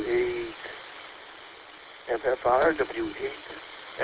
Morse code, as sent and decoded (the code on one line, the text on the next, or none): none